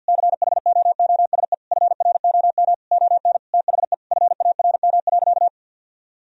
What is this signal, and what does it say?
Morse code, heard as a keyed tone, about 46 wpm, 700 Hz. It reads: CHOOSE FROM ON THE FARM.
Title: Sentences from Top 1000 Words - Review Part 4 - 50wpm